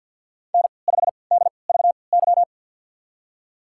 Morse code, heard as a keyed tone, about 51 wpm, spaced as for 28 wpm, 700 Hz. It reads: N5DVY
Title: USA Call Signs - 3x Character Spacing - 50wpm